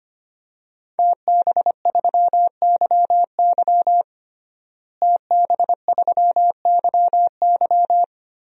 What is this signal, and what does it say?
Morse code, heard as a keyed tone, about 25 wpm, 700 Hz. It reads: TB3YY TB3YY
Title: DXCC-100 Call Signs - 5 Characters - Repeated 2x - 25wpm